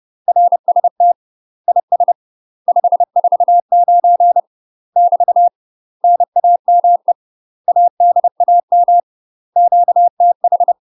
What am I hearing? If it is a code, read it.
Morse code, heard as a keyed tone, about 30 wpm, 700 Hz. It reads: RST IS 549 <BT> NAME ADAM QTH
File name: UK MASTER QSOs TRAINING / 30WPM / UK MASTER QSOs 0043